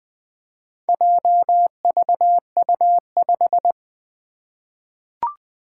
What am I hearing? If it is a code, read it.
Morse code, heard as a keyed tone, about 20 wpm, 700 Hz. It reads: JVU5
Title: Four Challenging Letter-Number Combinations 20wpm